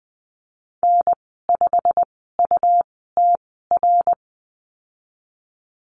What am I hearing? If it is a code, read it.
Morse code, heard as a keyed tone, about 20 wpm, spaced as for 14 wpm, 700 Hz. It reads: N5UTR
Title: 161 Lesson 26g - Letter g - Call Signs 20wpm